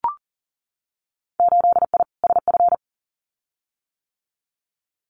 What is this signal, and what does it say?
Morse code, heard as a keyed tone, about 40 wpm, 700 Hz. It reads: 8I SF
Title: Winter Field Day Exchanges Encore 40wpm